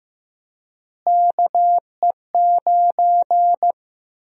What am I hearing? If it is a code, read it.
Morse code, heard as a keyed tone, about 15 wpm, 700 Hz. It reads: KE9
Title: USA Call Sign Prefixes - Repeated 2x - 15wpm